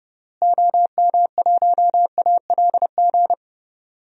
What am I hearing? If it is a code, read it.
Morse code, heard as a keyed tone, about 30 wpm, 700 Hz. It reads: OM1ALG
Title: DXCC-100 Call Signs - 6 Characters - Repeated 2x - 30wpm